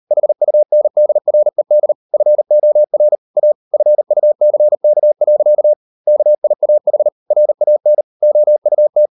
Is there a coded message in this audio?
Yes: Morse code from 0.1 to 9.1 s, heard as a keyed tone, about 38 wpm, 600 Hz. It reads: HUNDRED FOR A FUCK. KIAH RAN OUT